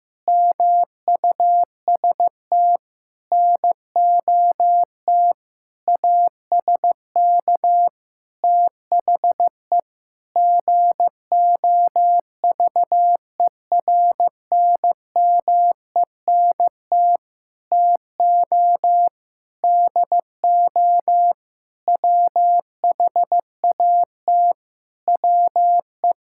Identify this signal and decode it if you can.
Morse code, heard as a keyed tone, about 15 wpm, 700 Hz. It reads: MUST NOT ASK THE GOVERNMENT TO DO WHAT WE
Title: Sentences from Top 1600 Words 15wpm